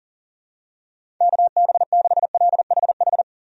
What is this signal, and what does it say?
Morse code, heard as a keyed tone, about 40 wpm, 690 Hz. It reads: KB6LHH
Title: USA Call Signs - 6 Characters - 40wpm